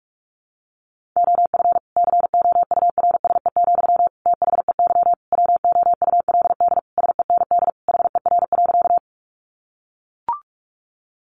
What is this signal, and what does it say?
Morse code, heard as a keyed tone, about 44 wpm, 700 Hz. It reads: OF COURSE, THEY WOULD SEND HER.
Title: Sentences from Top 1500 Words 45wpm